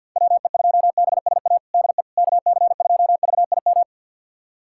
Morse code, heard as a keyed tone, about 52 wpm, 700 Hz. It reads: WE2BSA DE KC2VIM
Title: USA Callsign DE Callsign - 6 Characters - 50wpm